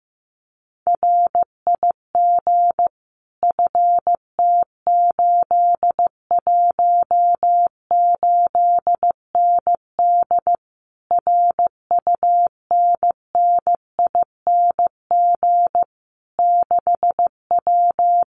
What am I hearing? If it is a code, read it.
Morse code, heard as a keyed tone, about 15 wpm, 700 Hz. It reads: RIG FT818ND RUNNING 6W
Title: Common QSO Phrases Encore 15wpm